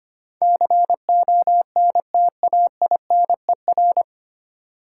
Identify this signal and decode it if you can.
Morse code, heard as a keyed tone, about 29 wpm, 700 Hz. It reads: CONTAINER
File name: Easy Words - Speed-Racing with Hints - 25wpm